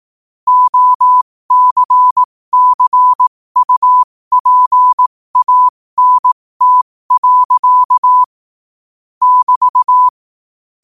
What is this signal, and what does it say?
Morse code, heard as a keyed tone, about 18 wpm, spaced as for 15 wpm, 1000 Hz. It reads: OCCUPANT. <BT>
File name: qotdfiles / 15wpm